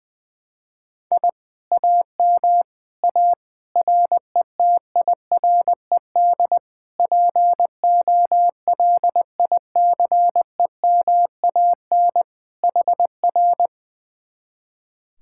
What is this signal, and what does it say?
Morse code, heard as a keyed tone, about 21 wpm, 700 Hz. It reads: I AM A RETIRED POLICEMAN HR
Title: Speed-Racing Common QSO Phrases Encore - 20wpm